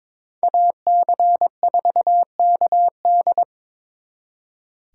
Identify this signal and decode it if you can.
Morse code, heard as a keyed tone, about 26 wpm, 700 Hz. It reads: AC4KD
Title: Speed Racing - USA 2x2 Call Signs 22wpm